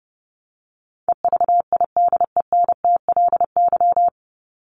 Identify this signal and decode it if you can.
Morse code, heard as a keyed tone, about 30 wpm, 700 Hz. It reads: EVIDENTLY
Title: LY Suffix - Top 100 Words with Base Words 30wpm